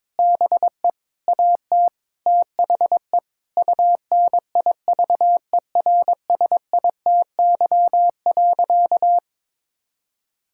Morse code, heard as a keyed tone, about 22 wpm, 700 Hz. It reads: BE AT THE UNIVERSITY.
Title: Sentences from Top 1000 Words - Review Part 5 - 22wpm